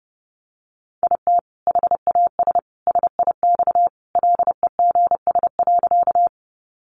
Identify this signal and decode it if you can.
Morse code, heard as a keyed tone, about 30 wpm, 700 Hz. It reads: IT HAS SIX LEGS.
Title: Additional Easy Phrases 30wpm